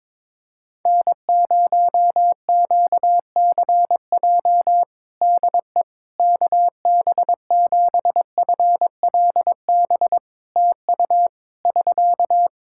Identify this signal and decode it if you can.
Morse code, heard as a keyed tone, about 21 wpm, 700 Hz. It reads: N0QCJ DE KB7FLB TU <SK>
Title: Practice QSO - Exchange 4 - Ending - 22wpm